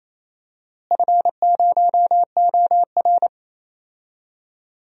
Morse code, heard as a keyed tone, about 28 wpm, 700 Hz. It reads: F0OR
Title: DXCC-100 Call Signs - 4 Characters - Repeated 2x - 28wpm